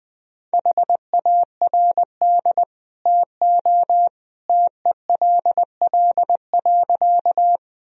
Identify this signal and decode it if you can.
Morse code, heard as a keyed tone, about 21 wpm, 700 Hz. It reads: HARD TO TELL.
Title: Sentences from Top 1000 Words - Review Part 1 - 20wpm